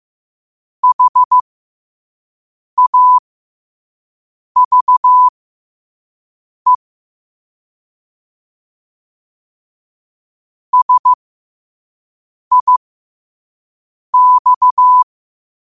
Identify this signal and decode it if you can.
Morse code, heard as a keyed tone, about 15 wpm, spaced as for 5 wpm, 1000 Hz. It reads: HAVE SIX
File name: qotdfiles / 5wpm